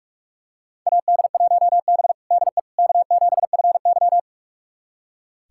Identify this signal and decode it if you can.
Morse code, heard as a keyed tone, about 41 wpm, 700 Hz. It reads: AD1B DE K7FY
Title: USA Callsign DE Callsign - 4 Characters - 45wpm